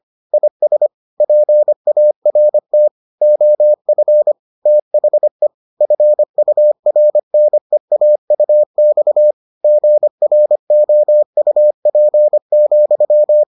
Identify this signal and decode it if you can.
Morse code, heard as a keyed tone, about 25 wpm, 600 Hz. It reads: IS PART OF THE FURNEAUX GROUP,